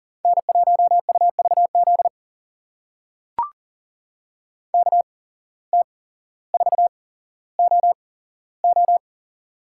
Morse code, heard as a keyed tone, about 40 wpm, spaced as for 17 wpm, 700 Hz. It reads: N1UVZ KT4OO
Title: USA Call Signs - 5 Characters - 8x Character Spacing - 40wpm